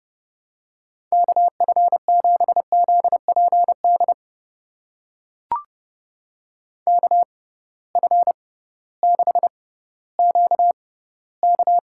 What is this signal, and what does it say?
Morse code, heard as a keyed tone, about 30 wpm, spaced as for 16 wpm, 700 Hz. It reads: KF7ZPD KF6QK
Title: USA Call Signs - 6 Characters - 6x Character Spacing - 30wpm